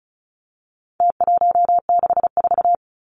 Morse code, heard as a keyed tone, about 34 wpm, 700 Hz. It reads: T164
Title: Numbers - Contest Style - 4-Digit 35wpm